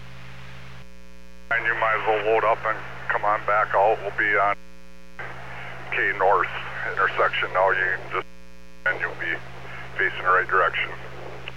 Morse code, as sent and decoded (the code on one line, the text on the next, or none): none